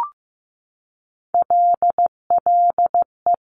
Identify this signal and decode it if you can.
Morse code, heard as a keyed tone, about 15 wpm, 700 Hz. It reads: LLE
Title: Top Three Letter Combinations in Words 15wpm